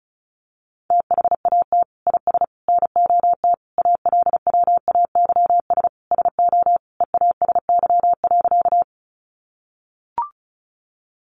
Morse code, heard as a keyed tone, about 34 wpm, 700 Hz. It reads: THAT IS NOT ALWAYS SO EASY.
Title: Sentences from Top 700 Words 35wpm